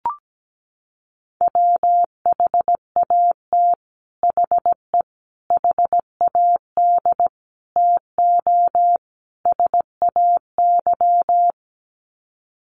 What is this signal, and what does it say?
Morse code, heard as a keyed tone, about 17 wpm, 700 Hz. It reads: WHAT HE HAD TO SAY
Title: Sets of 5 Words Encore2 17wpm